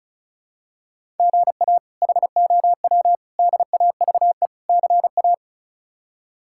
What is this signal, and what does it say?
Morse code, heard as a keyed tone, about 36 wpm, 700 Hz. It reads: GA HOW DAVE CA
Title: K1USN SST QSO Practice - 35wpm